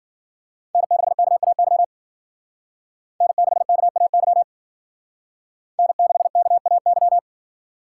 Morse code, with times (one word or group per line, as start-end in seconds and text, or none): 0.7-1.8 N6KAY
3.2-4.4 N6KAY
5.8-7.2 N6KAY